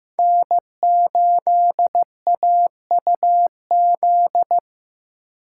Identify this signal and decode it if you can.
Morse code, heard as a keyed tone, about 15 wpm, 700 Hz. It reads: N8AUZ
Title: USA Call Signs - 4x Character Spacing - 15wpm